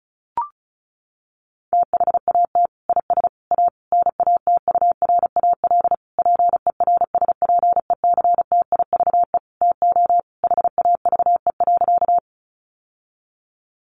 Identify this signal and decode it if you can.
Morse code, heard as a keyed tone, about 34 wpm, 700 Hz. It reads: THAT IS A NATURAL PERSPECTIVE TO HAVE.
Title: Sentences from Top 1200 Words 35wpm